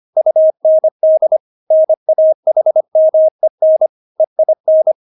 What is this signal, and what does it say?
Morse code, heard as a keyed tone, about 25 wpm, 630 Hz. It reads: UND NAHMEN EIN
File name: Witz6 25wpm